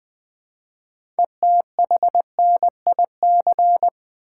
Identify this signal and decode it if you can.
Morse code, heard as a keyed tone, about 20 wpm, 700 Hz. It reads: ETHNIC
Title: Build A Word - 2 Letters At A Time - 6-Letter Words - 20wpm